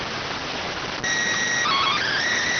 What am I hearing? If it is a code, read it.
SSTV (slow-scan television), PD 290.